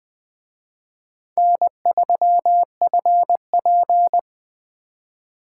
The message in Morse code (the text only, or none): N3FP